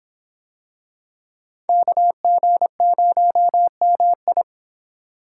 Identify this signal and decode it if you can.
Morse code, heard as a keyed tone, about 26 wpm, 700 Hz. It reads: KG0MI